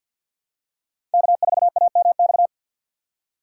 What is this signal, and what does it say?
Morse code, heard as a keyed tone, about 46 wpm, 700 Hz. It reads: K4AMX
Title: USA Call Signs - 5 Characters - 2x Character Spacing - 50wpm